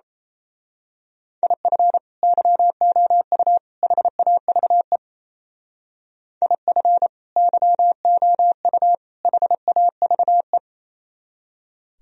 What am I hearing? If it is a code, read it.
Morse code, heard as a keyed tone, about 30 wpm, 700 Hz. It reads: IF YOU HAVE IF YOU HAVE
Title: Speed-Racing - Sets of 3 Words 28wpm